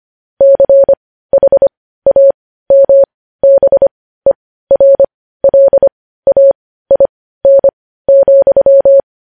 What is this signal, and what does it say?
Morse code, heard as a keyed tone, about 25 wpm, spaced as for 15 wpm, 550 Hz. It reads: CHAMBERLAIN,
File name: podcast / News headlines-1x-25x15